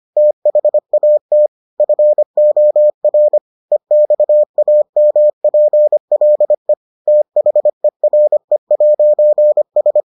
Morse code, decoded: THAT FOR EXAMPLE THERE'S

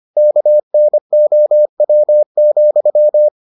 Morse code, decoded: KNOW,

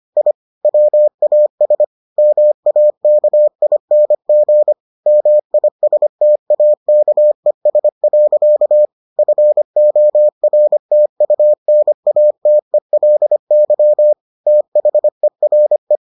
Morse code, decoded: I WAS MAKING MISTAKES. FORTUNATELY THERE

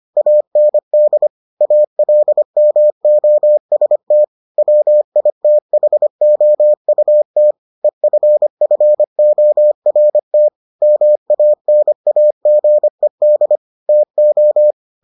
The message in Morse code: AND ALMOST WITHOUT EFFORT MANAGED TO

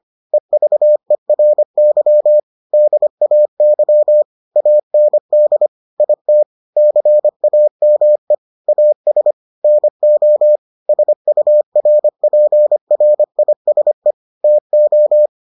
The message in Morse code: EVERY DAY AND IT CAME AS NO SURPRISE TO

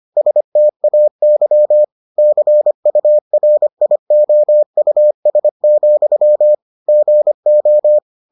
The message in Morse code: STAY CURIOUS, GO

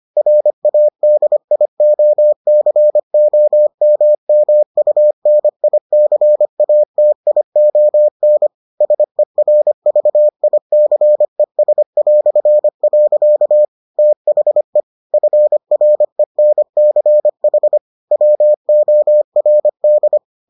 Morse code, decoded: RADIOCOMMUNICATION SERVICES". THE FRENCH WORD

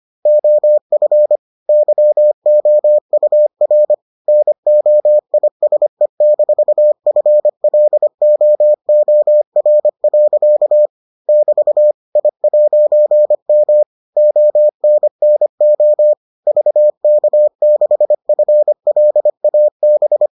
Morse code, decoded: OF YOUR NOISE-FLOOR. <BT> I'M ONNO VK6FLAB

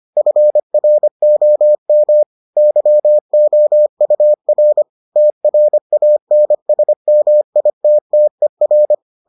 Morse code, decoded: FROM YOUR TRANSMITTER

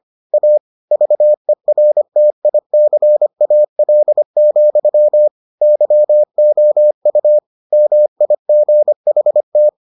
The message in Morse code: A VERTICAL, YOU MIGHT

